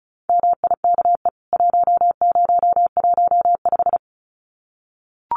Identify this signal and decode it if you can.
Morse code, heard as a keyed tone, about 34 wpm, 700 Hz. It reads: MIKE 1015
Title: CWT Exchange 35wpm